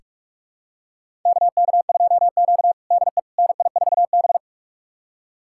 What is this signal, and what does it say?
Morse code, heard as a keyed tone, about 46 wpm, 700 Hz. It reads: KK2Q DE NI4B
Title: USA Callsign DE Callsign - 4 Characters - 45wpm